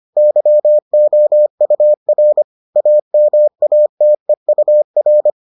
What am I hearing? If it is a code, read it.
Morse code, heard as a keyed tone, about 25 wpm, 600 Hz. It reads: YOUR AMATEUR